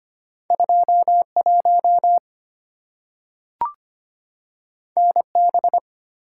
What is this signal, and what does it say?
Morse code, heard as a keyed tone, about 25 wpm, 700 Hz. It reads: 21 NB
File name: Two Letter-Number Combinations - 25wpm